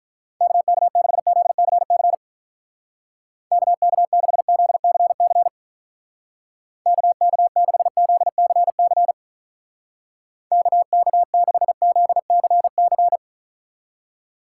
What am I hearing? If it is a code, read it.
Morse code, heard as a keyed tone, about 46 wpm, 700 Hz. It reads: KK6ZCC KK6ZCC KK6ZCC KK6ZCC